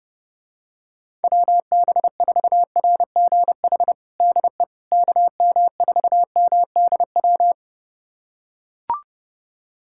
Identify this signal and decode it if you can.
Morse code, heard as a keyed tone, about 30 wpm, 700 Hz. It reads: WB4RGH DE KM4MDW
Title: USA Callsign DE Callsign 30wpm